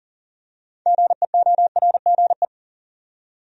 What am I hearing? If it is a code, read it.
Morse code, heard as a keyed tone, about 40 wpm, 700 Hz. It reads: GEORGE
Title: USA Common Names - 40wpm